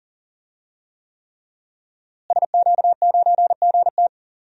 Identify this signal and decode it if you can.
Morse code, heard as a keyed tone, about 40 wpm, 700 Hz. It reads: SQ9GT
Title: DXCC-100 Call Signs - 5 Characters - Repeated 3x - 40wpm